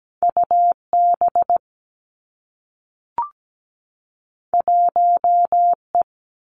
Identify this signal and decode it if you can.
Morse code, heard as a keyed tone, about 17 wpm, 700 Hz. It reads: U B 1 E
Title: Two Challenging Letter-Number Combinations 17wpm